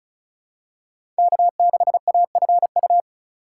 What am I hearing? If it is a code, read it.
Morse code, heard as a keyed tone, about 34 wpm, 700 Hz. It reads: K6AFU
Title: USA Call Signs - 5 Characters - 2x Character Spacing - 35wpm